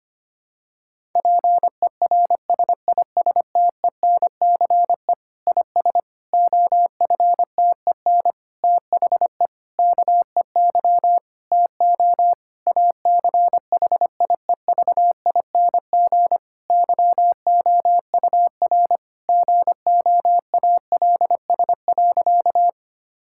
Morse code, heard as a keyed tone, about 25 wpm, 700 Hz. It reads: PERSISTENCE IS OFTEN THE KEY TO ACHIEVING YOUR GOALS.
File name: practice / 25wpm / Difficult Sentences with Hints - 25wpm